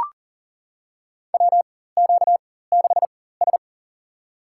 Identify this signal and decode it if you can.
Morse code, heard as a keyed tone, about 39 wpm, spaced as for 18 wpm, 700 Hz. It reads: WQ6S
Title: USA Call Signs - 4 Characters - 4x Character Spacing - 40wpm